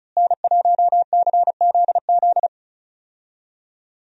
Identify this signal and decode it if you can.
Morse code, heard as a keyed tone, about 36 wpm, 700 Hz. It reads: N1CZZ
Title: USA Call Signs - 5 Characters - Repeated 3x - 35wpm